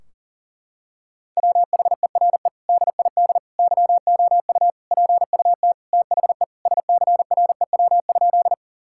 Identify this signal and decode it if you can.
Morse code, heard as a keyed tone, about 40 wpm, 700 Hz. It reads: WHERE DID YOU PUT THE SCREW?
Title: Sentences from Top 1600 Words 40wpm